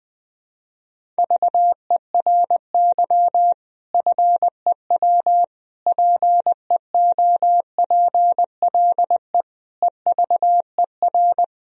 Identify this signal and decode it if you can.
Morse code, heard as a keyed tone, about 20 wpm, 700 Hz. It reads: VERY FEW PEOPLE EVER
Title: Sentences from Top 1000 Words - Review Part 1 - 20wpm